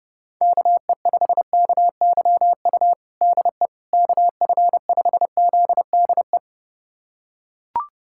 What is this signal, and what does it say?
Morse code, heard as a keyed tone, about 32 wpm, 700 Hz. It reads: KE5KYU DE KF5ZDE
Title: USA Callsign DE Callsign 30wpm